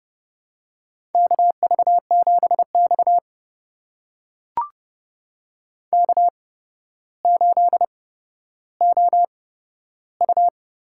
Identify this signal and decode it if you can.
Morse code, heard as a keyed tone, about 32 wpm, spaced as for 13 wpm, 700 Hz. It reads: KV7X K8OU
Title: USA Call Signs - 8x Character Spacing - 30wpm